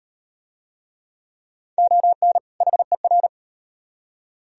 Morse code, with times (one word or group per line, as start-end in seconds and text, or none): 1.8-2.4 ON
2.6-3.3 HER